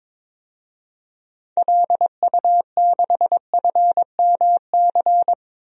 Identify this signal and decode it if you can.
Morse code, heard as a keyed tone, about 22 wpm, 700 Hz. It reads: LU6FMC
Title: DXCC-100 Call Signs - 6 Characters - 22wpm